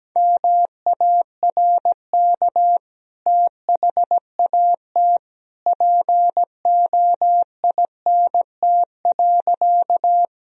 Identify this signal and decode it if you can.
Morse code, heard as a keyed tone, about 17 wpm, 700 Hz. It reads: MARK THAT POINT.
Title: Sentences from Top 1000 Words - Review Part 1 - 17wpm